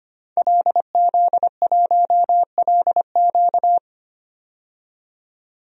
Morse code, heard as a keyed tone, about 25 wpm, 690 Hz. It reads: LZ1LQ
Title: DXCC-100 Call Signs - 5 Characters - 25wpm